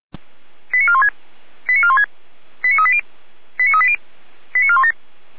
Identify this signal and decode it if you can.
Five-tone selective call (ZVEI), heard as quick runs of five tones, tones of about 70 ms. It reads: 89416 89416 89380 89380 89417
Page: SDR/ZVEI